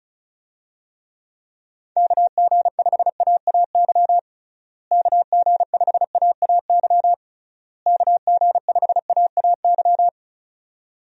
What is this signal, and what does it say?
Morse code, heard as a keyed tone, about 34 wpm, 700 Hz. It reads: KG5AAY KG5AAY KG5AAY
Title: USA Call Signs - 6 Characters - Repeated 3x - 35wpm